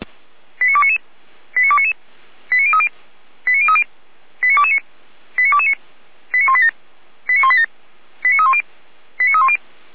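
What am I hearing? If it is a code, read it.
Five-tone selective call (ZVEI), heard as quick runs of five tones, tones of about 80 ms. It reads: 89299 89299 89030 89030 89229 89229 89187 89187 89310 89310
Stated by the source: ZVEI five-tone selective call 89075 received with an SDR.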